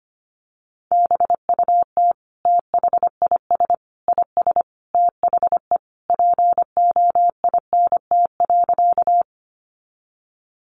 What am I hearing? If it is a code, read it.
Morse code, heard as a keyed tone, about 25 wpm, 700 Hz. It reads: BUT THIS IS THE POINT.
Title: Sentences from Top 200 Words 25wpm